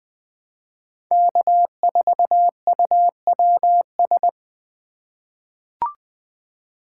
Morse code, heard as a keyed tone, about 21 wpm, 700 Hz. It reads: K4UWS
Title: USA Call Signs - 5 Characters - 3x Character Spacing - 20wpm